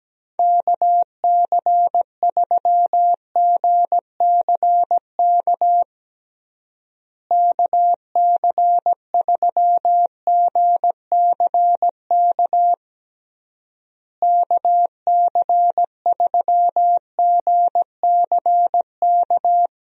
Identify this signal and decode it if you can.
Morse code, heard as a keyed tone, about 17 wpm, 700 Hz. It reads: KC3GCK KC3GCK KC3GCK
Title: USA Call Signs - 6 Characters - Repeated 3x - 17wpm